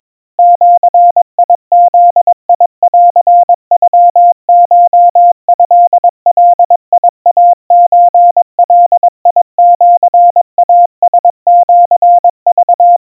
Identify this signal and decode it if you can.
Morse code (Japanese Wabun code), heard as a keyed tone, about 22 wpm, 700 Hz. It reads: ジブンノコトガイソガシイラシク
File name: kamyu 80 cpm 700 Hz Crispy